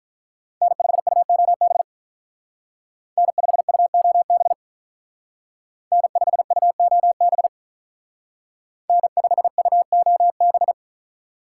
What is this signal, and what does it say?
Morse code, heard as a keyed tone, about 43 wpm, 700 Hz. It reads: N5UOB N5UOB N5UOB N5UOB
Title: Speed Racing - USA 1x3 Call Signs 35wpm